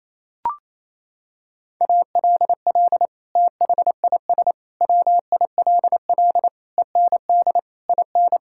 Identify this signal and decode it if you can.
Morse code, heard as a keyed tone, about 28 wpm, 700 Hz. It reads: ALL THIS WILL END IN
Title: Sentences from Top 1000 Words - Review Part 5 - 28wpm